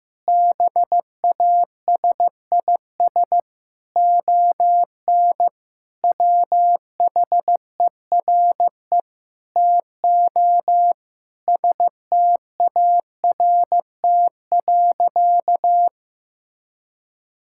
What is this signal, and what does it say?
Morse code, heard as a keyed tone, about 15 wpm, 700 Hz. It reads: BASIS ON WHERE TO START.
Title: Sentences from Top 1000 Words - Review Part 5 - 15wpm